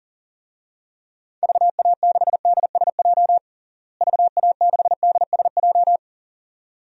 Morse code, heard as a keyed tone, about 40 wpm, 700 Hz. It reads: VA6DSJ VA6DSJ
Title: DXCC-100 Call Signs - 6 Characters - Repeated 2x - 40wpm